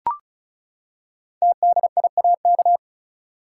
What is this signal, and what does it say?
Morse code, heard as a keyed tone, about 36 wpm, 700 Hz. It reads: TDIAK
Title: Code Groups - 5 Letters - 35wpm